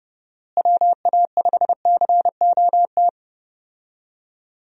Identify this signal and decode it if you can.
Morse code, heard as a keyed tone, about 30 wpm, 700 Hz. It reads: WA5COT
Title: USA Call Signs - 6 Characters - Repeated 3x - 30wpm